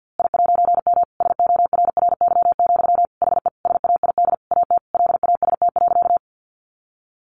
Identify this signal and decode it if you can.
Morse code, heard as a keyed tone, about 50 wpm, 700 Hz. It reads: I'M SORRY, HE SAID AT LAST.
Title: A Sound Of Thunder 50wpm